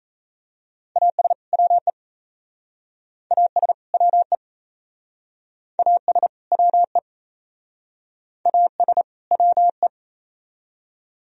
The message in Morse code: AS WE AS WE AS WE AS WE

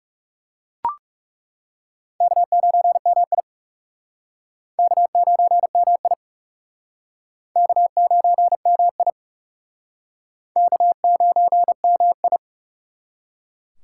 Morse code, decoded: K9MI K9MI K9MI K9MI

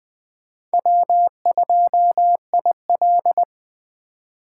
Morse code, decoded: W2IL

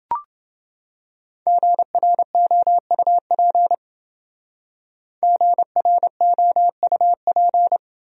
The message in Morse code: GROUP GROUP

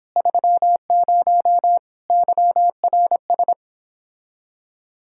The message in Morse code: 30 YRS

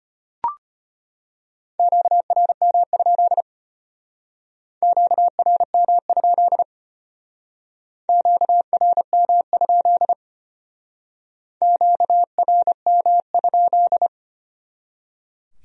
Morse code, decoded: QRM? QRM? QRM? QRM?

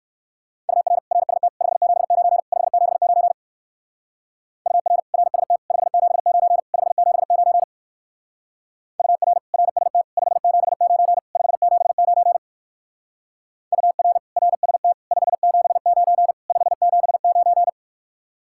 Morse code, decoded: UR RST 579 579 UR RST 579 579 UR RST 579 579 UR RST 579 579

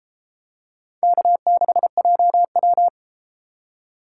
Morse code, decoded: K6JW

